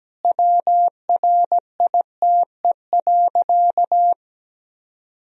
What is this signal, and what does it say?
Morse code, heard as a keyed tone, about 17 wpm, 700 Hz. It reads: WRITE.
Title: Sentences from Top 1000 Words - Review Part 1 - 17wpm